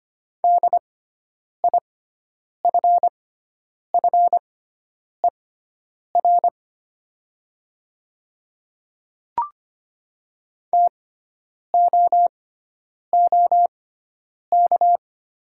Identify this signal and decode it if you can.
Morse code, heard as a keyed tone, about 25 wpm, spaced as for 9 wpm, 700 Hz. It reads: DIFFER TOOK
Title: 6x Character Spacing - Top 300 Words - 25wpm